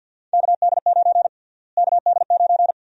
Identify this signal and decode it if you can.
Morse code, heard as a keyed tone, about 47 wpm, 700 Hz. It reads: KD9 KD9
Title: USA Call Sign Prefixes - Repeated 2x - 50wpm